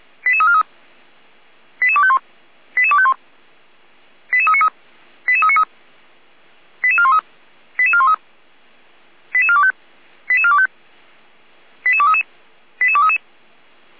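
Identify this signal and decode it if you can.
Five-tone selective call (ZVEI), heard as quick runs of five tones, tones of about 70 ms. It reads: 88343 88251 88251 88393 88393 88413 88413 88425 88425 88233 88233
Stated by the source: ZVEI five-tone call 88288 received by SDR.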